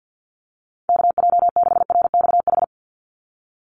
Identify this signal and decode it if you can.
Morse code, heard as a keyed tone, about 50 wpm, 700 Hz. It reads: KJ6RXH
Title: USA Call Signs 50wpm